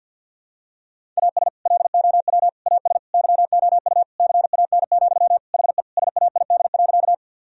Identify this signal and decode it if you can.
Morse code, heard as a keyed tone, about 48 wpm, 700 Hz. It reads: AS LOW AS YOU CAN, HE SAID.
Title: Sentences from Top 1000 Words - Review Part 1 - 50wpm